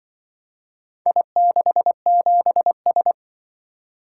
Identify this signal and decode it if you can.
Morse code, heard as a keyed tone, about 24 wpm, 700 Hz. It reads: I67S
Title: Speed-Racing - Four Letter-Number Combinations - 20wpm